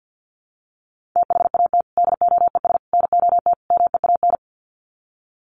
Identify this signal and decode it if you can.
Morse code, heard as a keyed tone, about 50 wpm, 700 Hz. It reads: THAT DOES NOT MEAN
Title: Sets of 4 Words 50wpm